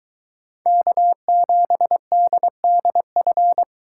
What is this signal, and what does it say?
Morse code, heard as a keyed tone, about 22 wpm, 700 Hz. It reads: K7DDF